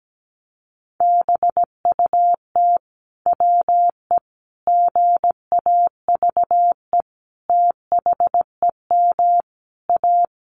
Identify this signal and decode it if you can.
Morse code, heard as a keyed tone, about 17 wpm, 700 Hz. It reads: BUT WE GAVE THEM A
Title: Sentences from Top 1200 Words 17wpm